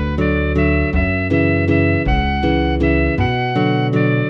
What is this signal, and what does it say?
There is no code.